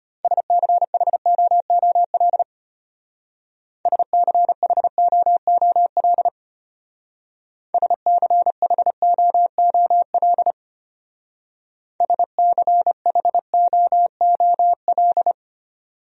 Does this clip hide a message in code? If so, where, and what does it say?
Yes: Morse code from 0.2 to 15.3 s, heard as a keyed tone, about 32 wpm, 700 Hz. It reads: SCHOOL SCHOOL SCHOOL SCHOOL